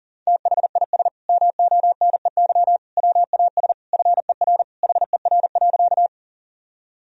Morse code, heard as a keyed tone, about 40 wpm, 700 Hz. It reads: THIS MONEY WAS FER HER.
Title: Sentences from Top 400 Words and Abbreviations 40wpm